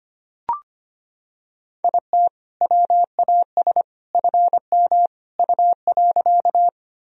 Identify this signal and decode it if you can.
Morse code, heard as a keyed tone, about 25 wpm, 700 Hz. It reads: IT WAS FM U.
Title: Sentences from Top 100 Words and Abbreviations 25wpm